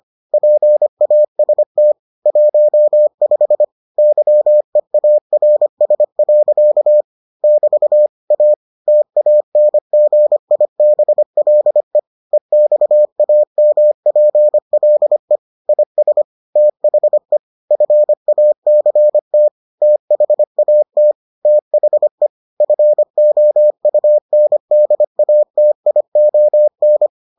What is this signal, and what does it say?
Morse code, heard as a keyed tone, about 25 wpm, 600 Hz. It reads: PAST 15 YEARS. <BT> A TANGIBLE EXAMPLE IS THE FACT THAT THE FOUNDATION